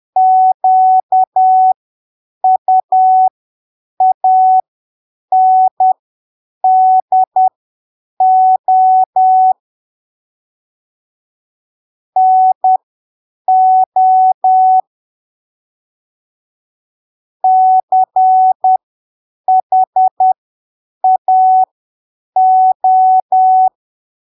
Morse code, heard as a keyed tone, about 10 wpm, spaced as for 7 wpm, 750 Hz. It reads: QUANDO NO CHAO
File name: arquivos / texto / texto 21 10ppm